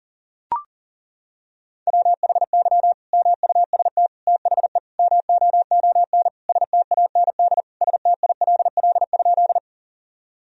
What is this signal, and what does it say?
Morse code, heard as a keyed tone, about 40 wpm, 700 Hz. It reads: WHY MUST THE MOON STAND STILL?
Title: Sentences from Top 1000 Words - Review Part 3 - 40wpm